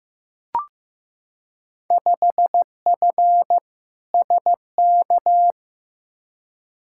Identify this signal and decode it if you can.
Morse code, heard as a keyed tone, about 16 wpm, 700 Hz. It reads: 5F SK
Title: Field Day Exchanges Encore - 15wpm